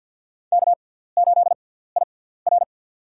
Morse code, heard as a keyed tone, about 46 wpm, spaced as for 17 wpm, 700 Hz. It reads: K8IR
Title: USA Call Signs - 4 Characters - 6x Character Spacing - 50wpm